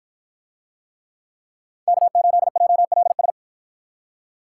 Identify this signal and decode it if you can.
Morse code, heard as a keyed tone, about 53 wpm, 700 Hz. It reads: K8JLS